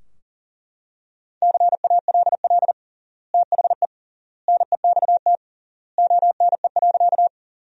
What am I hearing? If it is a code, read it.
Morse code, heard as a keyed tone, about 40 wpm, 700 Hz. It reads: CALL THE NEXT ONE.
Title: 3x Word-Spacing - Sentences from Top 300 Words - 40wpm